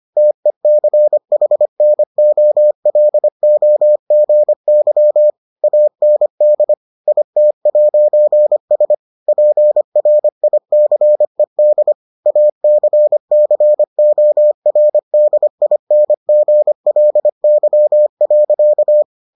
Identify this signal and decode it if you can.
Morse code, heard as a keyed tone, about 25 wpm, 600 Hz. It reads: TECHNOLOGY AND IT'S PRICED ACCORDINGLY.